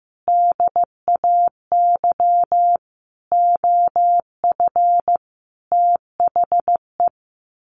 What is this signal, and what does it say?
Morse code, heard as a keyed tone, about 15 wpm, 700 Hz. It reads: DAY OF THE
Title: Sets of 5 Words Encore2 15wpm